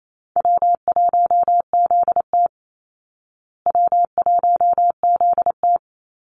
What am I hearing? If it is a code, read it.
Morse code, heard as a keyed tone, about 28 wpm, 700 Hz. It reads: W1ZT W1ZT
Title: Call Signs Repeated 2x - 28wpm